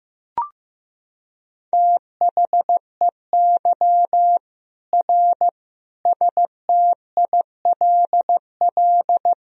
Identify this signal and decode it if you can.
Morse code, heard as a keyed tone, about 16 wpm, 700 Hz. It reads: THEY R STILL
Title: Sentences from Top 400 Words and Abbreviations 15wpm